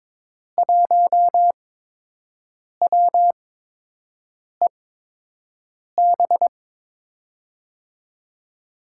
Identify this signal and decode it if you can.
Morse code, heard as a keyed tone, about 22 wpm, spaced as for 6 wpm, 700 Hz. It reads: 1WEB
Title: USA Call Signs - 6 Characters - 8x Character Spacing - 22wpm